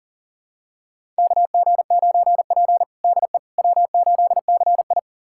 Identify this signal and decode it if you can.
Morse code, heard as a keyed tone, about 40 wpm, 700 Hz. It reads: KG9P DE W8CI
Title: USA Callsign DE Callsign - 4 Characters - 40wpm